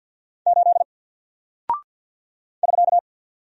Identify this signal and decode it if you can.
Morse code, heard as a keyed tone, about 48 wpm, 700 Hz. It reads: 8 <SK>
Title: Single Character (Letter:Number:Punctuation:Prosign) Alt 50wpm